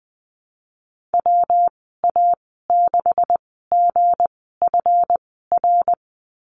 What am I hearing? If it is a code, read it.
Morse code, heard as a keyed tone, about 19 wpm, 700 Hz. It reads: W A 6 G F R
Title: 207 Lesson 33g - Letter j - Call Signs 20wpm